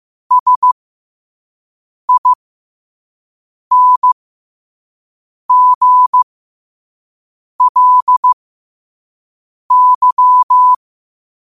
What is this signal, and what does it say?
Morse code, heard as a keyed tone, about 15 wpm, spaced as for 5 wpm, 1000 Hz. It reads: SINGLY
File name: qotdfiles / 5wpm